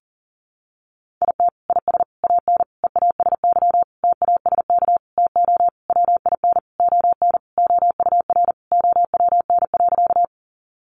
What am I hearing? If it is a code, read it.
Morse code, heard as a keyed tone, about 40 wpm, 700 Hz. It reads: IT IS AN EASY TASK TO WIN ON OUR OWN.
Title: Sentences from Top 1500 Words 40wpm